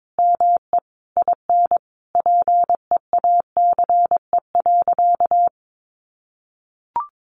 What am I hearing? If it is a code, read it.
Morse code, heard as a keyed tone, about 22 wpm, 700 Hz. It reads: ME IN PEACE.
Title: Sentences from Top 1500 Words 22wpm